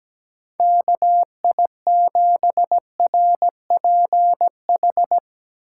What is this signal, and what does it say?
Morse code, heard as a keyed tone, about 17 wpm, 700 Hz. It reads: KI7RPH
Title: USA Call Signs - 6 Characters - 4x Character Spacing - 17wpm